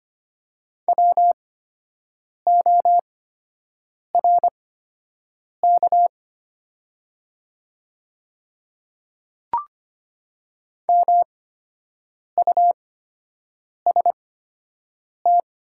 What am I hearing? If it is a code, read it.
Morse code, heard as a keyed tone, about 25 wpm, spaced as for 7 wpm, 690 Hz. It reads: WORK MUST
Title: 8x Character Spacing - Top 200 Words - 25wpm